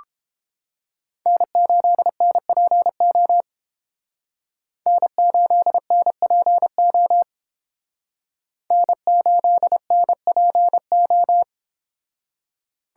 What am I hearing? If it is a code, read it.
Morse code, heard as a keyed tone, about 30 wpm, 700 Hz. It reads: N8NPO N8NPO N8NPO